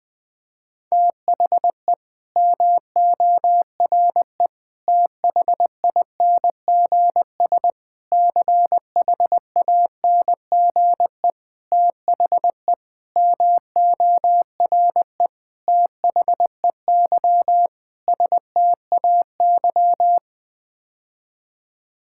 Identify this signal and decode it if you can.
Morse code, heard as a keyed tone, about 20 wpm, 700 Hz. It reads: THE MORE THINGS CHANGE THE MORE THEY STAY
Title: Sets of 8 Words - 20wpm